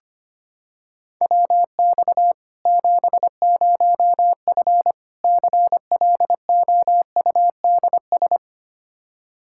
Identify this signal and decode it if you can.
Morse code, heard as a keyed tone, about 25 wpm, 700 Hz. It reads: WX 70F CLOUDS
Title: K1USN SST QSO Practice 25wpm